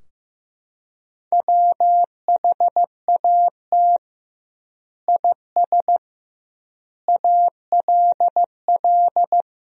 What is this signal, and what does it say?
Morse code, heard as a keyed tone, about 15 wpm, 700 Hz. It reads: WHAT IS ALL
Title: 2x Word-Spacing - Sentences from Top 100 Words - 15wpm